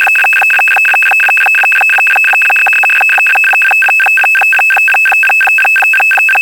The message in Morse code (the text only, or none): none